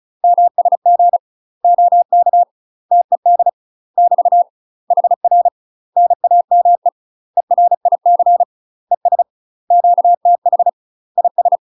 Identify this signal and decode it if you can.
Morse code, heard as a keyed tone, about 36 wpm, 700 Hz. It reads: MSG OK TED <BT> HR NAME ERIC ES QTH IS
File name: UK MASTER QSOs TRAINING / 35WPM / UK MASTER QSOs 0035